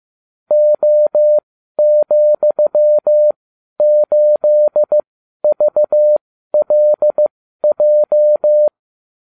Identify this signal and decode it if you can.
Morse code, heard as a keyed tone, about 15 wpm, spaced as for 12 wpm, 600 Hz. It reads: O,8VLJ